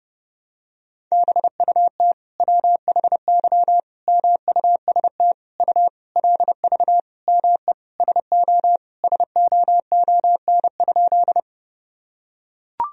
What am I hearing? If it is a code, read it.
Morse code, heard as a keyed tone, about 30 wpm, 700 Hz. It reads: BUT WHY MUST U LV ME SO SOON?
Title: Sentences from Top 400 Words and Abbreviations 30wpm